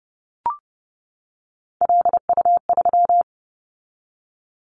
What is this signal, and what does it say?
Morse code, heard as a keyed tone, about 30 wpm, 700 Hz. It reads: LU3
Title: Callsign Prefixes for Top 50 DXCC - 30wpm